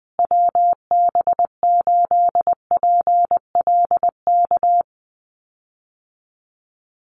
Morse code, heard as a keyed tone, about 19 wpm, 700 Hz. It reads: WB8PLK
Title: USA Call Signs Repeated 2x 20wpm